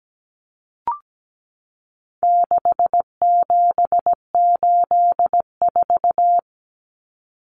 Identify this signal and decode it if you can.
Morse code, heard as a keyed tone, about 17 wpm, 700 Hz. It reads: 6784
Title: Common Cut Numbers - 4-Digit 17wpm